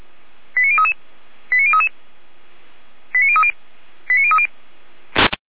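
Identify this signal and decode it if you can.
Five-tone selective call (ZVEI), heard as quick runs of five tones, tones of about 70 ms. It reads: 89033 89033 89030 89030
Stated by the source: ZVEI five-tone selective call received by SDR, address 89099, from a server of Feuerwehr Orlamünde (fire brigade).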